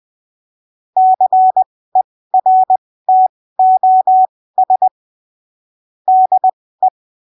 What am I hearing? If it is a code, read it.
Morse code, heard as a keyed tone, about 20 wpm, spaced as for 15 wpm, 750 Hz. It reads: CERTOS DE